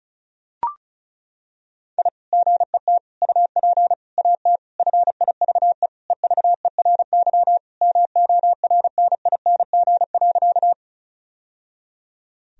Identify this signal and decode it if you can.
Morse code, heard as a keyed tone, about 36 wpm, 700 Hz. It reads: I GET UP AT FIVE EVERY MORNING.
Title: Sentences from Top 1000 Words - Review Part 2 - 35wpm